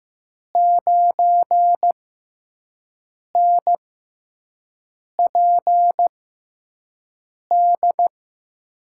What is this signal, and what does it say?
Morse code, heard as a keyed tone, about 15 wpm, spaced as for 5 wpm, 700 Hz. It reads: 9NPD
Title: USA Call Signs - 5 Characters - 6x Character Spacing - 15wpm